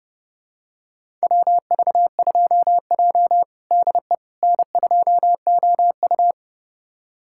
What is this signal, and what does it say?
Morse code, heard as a keyed tone, about 29 wpm, 700 Hz. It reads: WV2J DE N2OU
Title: USA Callsign DE Callsign - 4 Characters - 30wpm